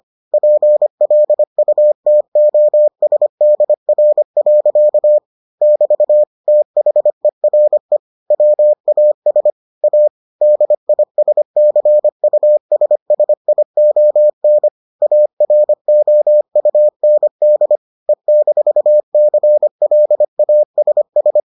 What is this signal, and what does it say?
Morse code, heard as a keyed tone, about 25 wpm, 600 Hz. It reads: PLUTOSDR. <BT> THERE WAS A DISCUSSION AROUND E-CLASS